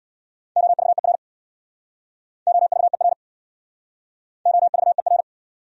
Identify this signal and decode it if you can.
Morse code, heard as a keyed tone, about 66 wpm, 700 Hz. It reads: OVER OVER OVER